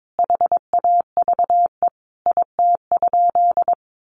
Morse code, heard as a keyed tone, about 22 wpm, 700 Hz. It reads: HAVE IT?
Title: Sentences from Top 100 Words 22wpm